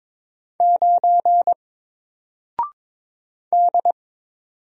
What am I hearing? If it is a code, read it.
Morse code, heard as a keyed tone, about 22 wpm, 700 Hz. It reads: 9 D
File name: Single-Letter-Number-22wpm